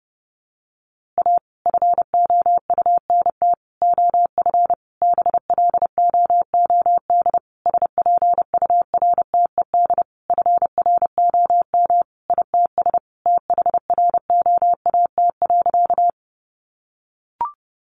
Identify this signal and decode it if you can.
Morse code, heard as a keyed tone, about 30 wpm, 700 Hz. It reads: A FOUNT OF BLOOD SPURTED FROM ITS THROAT.
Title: A Sound Of Thunder 30wpm